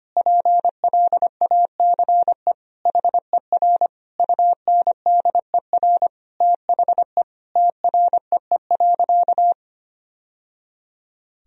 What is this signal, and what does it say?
Morse code, heard as a keyed tone, about 25 wpm, 700 Hz. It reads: PLACE HER UNDER THE TREE.
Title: Sentences from Top 1000 Words - Review Part 1 - 25wpm